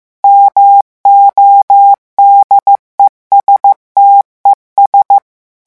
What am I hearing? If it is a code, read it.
Morse code, heard as a keyed tone, about 15 wpm, 800 Hz. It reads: MODESTES
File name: morse mp3 / revision9 citations humoristiques vitesse 14 mots minute